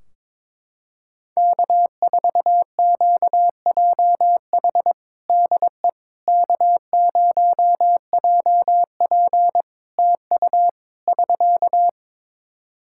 Morse code, heard as a keyed tone, about 22 wpm, 700 Hz. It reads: K4QJH DE K0JP TU <SK>
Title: Practice QSO - Exchange 4 - Ending - 22wpm